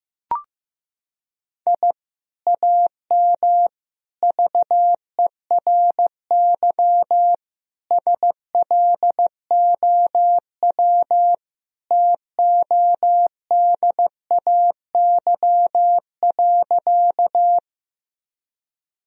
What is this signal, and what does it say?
Morse code, heard as a keyed tone, about 15 wpm, 700 Hz. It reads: I AM VERY SLOW TODAY.
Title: Sentences from Top 1000 Words - Review Part 2 - 15wpm